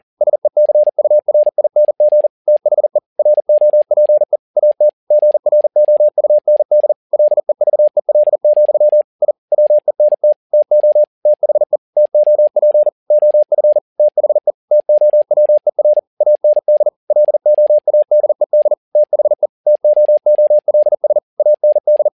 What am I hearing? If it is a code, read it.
Morse code, heard as a keyed tone, about 40 wpm, 600 Hz. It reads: SECURING THE ROPE AT GROUND LEVEL, I WENT TO THE TOP OF THE TOWER AND LOADED THE TOOLS AND
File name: learning center / tower trouble / tower touble 40wpm